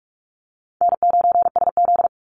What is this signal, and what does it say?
Morse code, heard as a keyed tone, about 44 wpm, 700 Hz. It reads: N9SZ
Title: Easy Call Signs - 45wpm